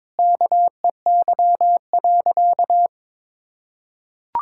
Morse code, heard as a keyed tone, about 22 wpm, 700 Hz. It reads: KEY.
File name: Sentences with 7-Word Phrases - 22wpm